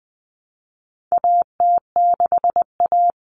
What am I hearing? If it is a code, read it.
Morse code, heard as a keyed tone, about 20 wpm, 700 Hz. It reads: AT6A
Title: Common Cut Numbers - 4-Digit 20wpm